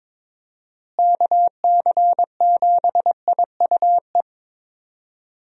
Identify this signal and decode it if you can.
Morse code, heard as a keyed tone, about 22 wpm, 700 Hz. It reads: KC7IUE
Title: USA Call Signs - 6 Characters - Repeated 3x - 22wpm